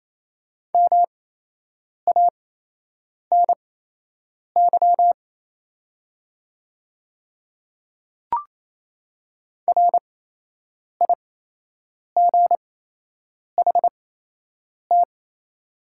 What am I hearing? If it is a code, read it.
Morse code, heard as a keyed tone, about 28 wpm, spaced as for 8 wpm, 700 Hz. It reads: MANY RIGHT
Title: 8x Character Spacing - Top 200 Words - 28wpm